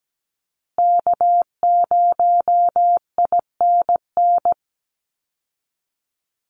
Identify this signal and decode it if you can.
Morse code, heard as a keyed tone, about 17 wpm, 700 Hz. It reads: K0INN
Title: Call Signs Repeated 2x - 17wpm